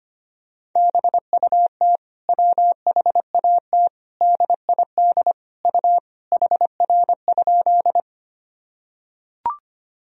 Morse code, heard as a keyed tone, about 25 wpm, 700 Hz. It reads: BUT WHAT DID U HR?
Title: Sentences from Top 300 Words and Abbreviations 25wpm